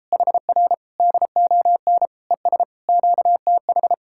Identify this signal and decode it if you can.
Morse code, heard as a keyed tone, about 33 wpm, 700 Hz. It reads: HR DON ES QTH